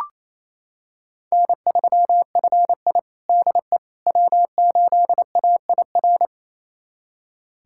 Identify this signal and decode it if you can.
Morse code, heard as a keyed tone, about 28 wpm, 700 Hz. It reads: N3FI DE W8AIR
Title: USA Callsign DE Callsign 28wpm